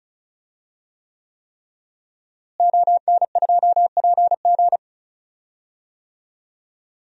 Morse code, heard as a keyed tone, about 34 wpm, 700 Hz. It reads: ON2PG
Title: DXCC-100 Call Signs - 5 Characters - Repeated 3x - 35wpm